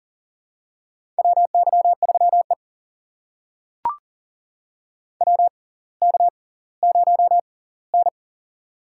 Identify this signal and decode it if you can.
Morse code, heard as a keyed tone, about 40 wpm, spaced as for 21 wpm, 700 Hz. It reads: WY3E WK0N
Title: USA Call Signs - 4 Characters - 6x Character Spacing - 40wpm